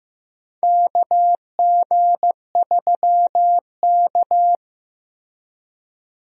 Morse code, heard as a keyed tone, about 15 wpm, 700 Hz. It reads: KG3K
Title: USA Call Signs - 4 Characters - Repeated 2x - 15wpm